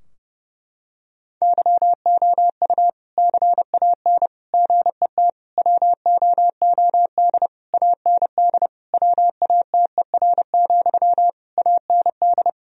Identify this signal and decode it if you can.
Morse code, heard as a keyed tone, about 30 wpm, 700 Hz. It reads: YOU CAN GET WOOD AND WATER, AND
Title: Sentences from Top 1000 Words - Review Part 2 - 30wpm